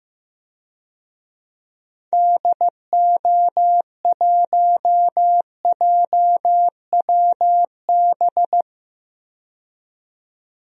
Morse code, heard as a keyed tone, about 15 wpm, 700 Hz. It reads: DO1JWB
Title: DXCC-100 Call Signs - 6 Characters - 15wpm